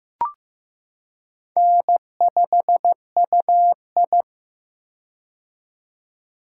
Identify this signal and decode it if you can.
Morse code, heard as a keyed tone, about 15 wpm, 700 Hz. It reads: N5UI
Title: K1USN SST QSO Practice - 15wpm